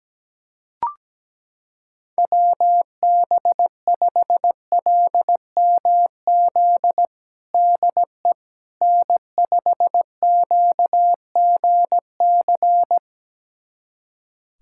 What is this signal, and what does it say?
Morse code, heard as a keyed tone, about 17 wpm, 700 Hz. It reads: WB5LMZ DE N5QGC
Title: USA Callsign DE Callsign 17wpm